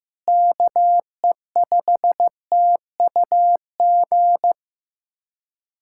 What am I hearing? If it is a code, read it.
Morse code, heard as a keyed tone, about 15 wpm, 700 Hz. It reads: KE5TUG
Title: USA Callsign DE Callsign - 6 Characters - 15wpm